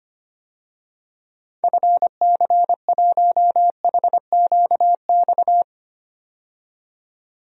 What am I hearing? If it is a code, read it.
Morse code, heard as a keyed tone, about 25 wpm, 700 Hz. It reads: FC1HQX
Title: DXCC-100 Call Signs - 6 Characters - 25wpm